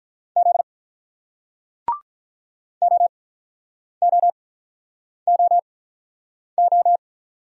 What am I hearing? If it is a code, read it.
Morse code, heard as a keyed tone, about 47 wpm, spaced as for 9 wpm, 700 Hz. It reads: Z OOOO